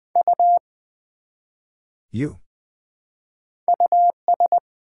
Morse code, heard as a keyed tone, about 20 wpm, 700 Hz. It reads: U US
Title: Build A Word - 1 Letter At A Time - 5-Letter Words - 20wpm